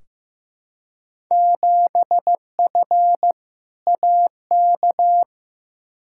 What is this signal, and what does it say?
Morse code, heard as a keyed tone, about 15 wpm, 700 Hz. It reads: 7F AK
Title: Field Day Exchanges Encore - 15wpm